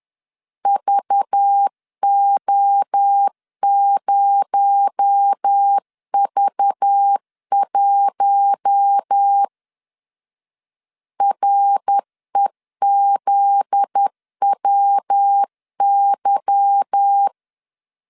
Morse code, heard as a keyed tone, about 10 wpm, 800 Hz. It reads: VO0V1 REZWY
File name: CW / DET / 1 9 1500C S C 10wpm 4557